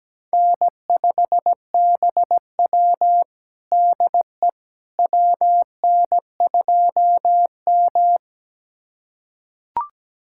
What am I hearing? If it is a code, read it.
Morse code, heard as a keyed tone, about 17 wpm, 700 Hz. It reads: N5BW DE WN2M
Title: USA Callsign DE Callsign - 4 Characters - 17wpm